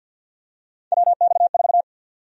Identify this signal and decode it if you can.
Morse code, heard as a keyed tone, about 53 wpm, 700 Hz. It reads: WX4
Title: USA Call Sign Prefixes - Repeated 2x - 50wpm